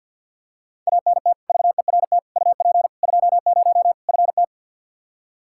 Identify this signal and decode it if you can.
Morse code, heard as a keyed tone, about 47 wpm, 700 Hz. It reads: ANT VERT UP 20 FT
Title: Practice QSO - Exchange 2 - Info - 50wpm